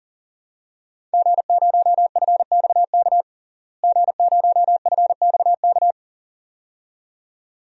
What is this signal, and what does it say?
Morse code, heard as a keyed tone, about 40 wpm, 700 Hz. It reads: G0FXK G0FXK
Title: DXCC-100 Call Signs - 5 Characters - Repeated 2x - 40wpm